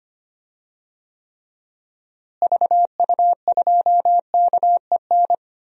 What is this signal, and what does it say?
Morse code, heard as a keyed tone, about 25 wpm, 700 Hz. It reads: VU2KEN
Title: DXCC-100 Call Signs - 6 Characters - Repeated 3x - 25wpm